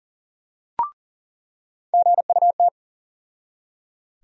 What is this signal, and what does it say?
Morse code, heard as a keyed tone, about 40 wpm, 700 Hz. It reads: GUT
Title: USA Call Sign Postfixes - 3 Letters 40wpm